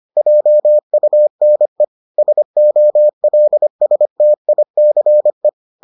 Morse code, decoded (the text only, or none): JUNE SOLSTICE